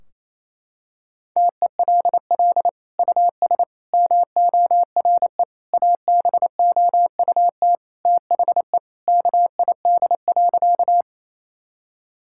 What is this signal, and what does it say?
Morse code, heard as a keyed tone, about 28 wpm, 700 Hz. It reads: TELL US MORE ABOUT THE KID.
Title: Sentences from Top 1600 Words 28wpm